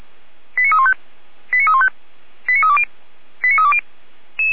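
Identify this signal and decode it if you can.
Five-tone selective call (ZVEI), heard as quick runs of five tones, tones of about 70 ms. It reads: 89315 89315 89320 89320
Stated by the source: ZVEI five-tone selective call received with an SDR, archived by Feuerwehr Orlamuende.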